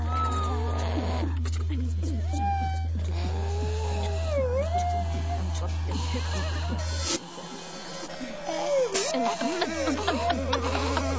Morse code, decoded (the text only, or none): none